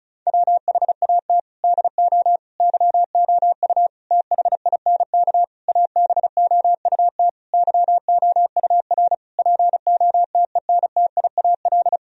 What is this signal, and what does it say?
Morse code, heard as a keyed tone, about 34 wpm, 700 Hz. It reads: WHAT DO YOU THINK ABOUT YOUR POTENTIAL
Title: Sentences from Top 1000 Words - Review Part 4 - 35wpm